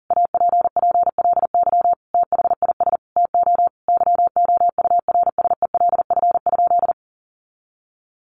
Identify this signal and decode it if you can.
Morse code, heard as a keyed tone, about 40 wpm, 700 Hz. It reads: APPLY THIS TO YOURSELF?
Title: Sentences from Top 1200 Words 40wpm